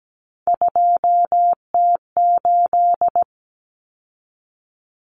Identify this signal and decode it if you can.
Morse code, heard as a keyed tone, about 17 wpm, 700 Hz. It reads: 2T8
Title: Common Cut Numbers - 3-Digit 17wpm